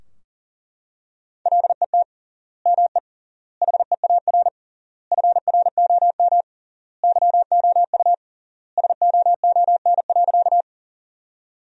Morse code, heard as a keyed tone, about 40 wpm, 700 Hz. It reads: LET ME HEAR FROM YOU SOON.
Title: 3x Word-Spacing - Sentences from Top 400 Words - 40wpm